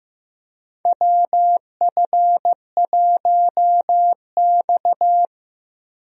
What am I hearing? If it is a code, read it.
Morse code, heard as a keyed tone, about 15 wpm, 700 Hz. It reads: WF1X
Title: USA Call Signs - 4 Characters - Repeated 2x - 15wpm